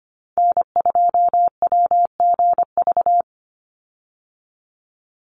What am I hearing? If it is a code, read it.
Morse code, heard as a keyed tone, about 25 wpm, 700 Hz. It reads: N2WGV
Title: USA Call Signs 25wpm